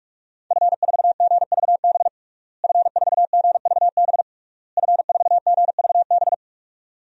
Morse code, heard as a keyed tone, about 46 wpm, 700 Hz. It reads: F4GVB F4GVB F4GVB
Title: DXCC-100 Call Signs - 5 Characters - Repeated 3x - 45wpm